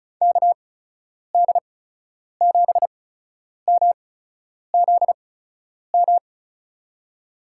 Morse code, heard as a keyed tone, about 36 wpm, spaced as for 10 wpm, 700 Hz. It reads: KD7MZM
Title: USA Call Signs - 8x Character Spacing - 35wpm